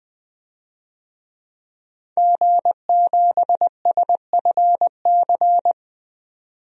Morse code, heard as a keyed tone, about 20 wpm, 700 Hz. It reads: G7SFC
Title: DXCC-100 Call Signs - 5 Characters - 20wpm